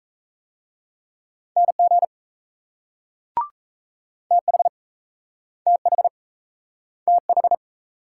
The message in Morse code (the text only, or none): NG TH TH TH